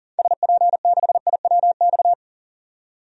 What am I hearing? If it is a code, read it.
Morse code, heard as a keyed tone, about 40 wpm, 700 Hz. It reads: SP6IWX
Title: DXCC-100 Call Signs - 6 Characters - 40wpm